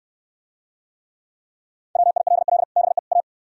Morse code, heard as a keyed tone, about 68 wpm, 700 Hz. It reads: WILL BE A